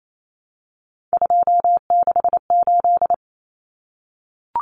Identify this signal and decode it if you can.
Morse code, heard as a keyed tone, about 28 wpm, 700 Hz. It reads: 268
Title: Common Cut Numbers - 3-Digit 28wpm